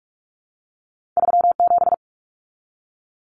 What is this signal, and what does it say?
Morse code, heard as a keyed tone, about 46 wpm, 700 Hz. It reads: 37
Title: Numbers - 2-Digit 45wpm